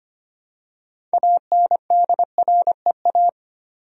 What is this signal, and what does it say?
Morse code, heard as a keyed tone, about 25 wpm, 700 Hz. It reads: ANDREA
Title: USA Common Names - 25wpm